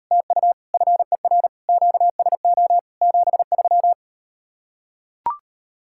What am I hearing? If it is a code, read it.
Morse code, heard as a keyed tone, about 39 wpm, 700 Hz. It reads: TU FER QSO 73